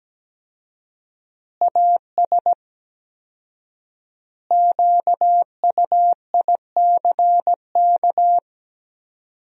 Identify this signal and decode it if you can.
Morse code, heard as a keyed tone, about 17 wpm, 700 Hz. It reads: AS QUICK
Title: 4x Word-Spacing - Sentences from Top 500 Words - 17wpm